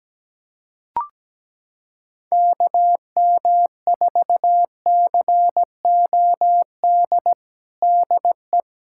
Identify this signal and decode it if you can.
Morse code, heard as a keyed tone, about 17 wpm, 700 Hz. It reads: KM4COD DE
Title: USA Callsign DE Callsign - 6 Characters - 17wpm